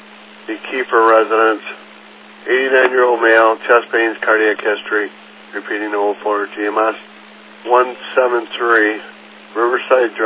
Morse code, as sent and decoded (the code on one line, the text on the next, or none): none